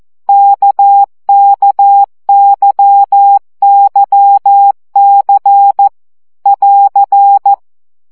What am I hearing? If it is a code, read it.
Morse code, heard as a keyed tone, about 14 wpm, 800 Hz. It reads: KKYYC <AR>
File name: CW gyakorlas k c y 12 baud